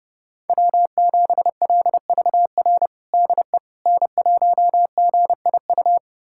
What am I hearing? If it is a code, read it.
Morse code, heard as a keyed tone, about 30 wpm, 700 Hz. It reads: W7LVR DE N1GIU
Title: USA Callsign DE Callsign - 5 Characters - 30wpm